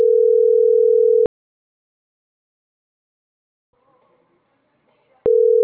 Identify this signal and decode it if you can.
Ringback tone (North America), 440 + 480 Hz.